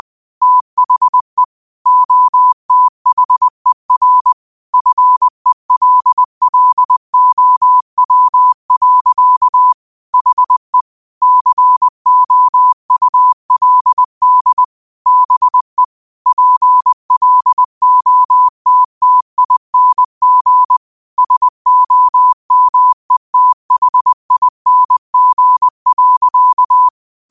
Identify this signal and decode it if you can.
Morse code, heard as a keyed tone, about 20 wpm, 1000 Hz. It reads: THE OTHER FELLOW. HE COULD BE PLOTTING SOMETHING.
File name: qotdfiles / 20wpm